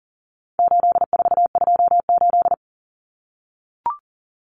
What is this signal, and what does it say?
Morse code, heard as a keyed tone, about 40 wpm, 700 Hz. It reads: 8428
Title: Numbers - 4-Digit 40wpm